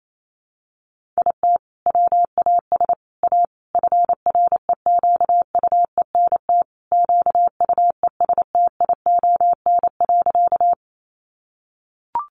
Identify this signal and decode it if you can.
Morse code, heard as a keyed tone, about 28 wpm, 700 Hz. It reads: IT WAS A FREQUENT QUESTION.
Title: Sentences from Top 1400 Words 28wpm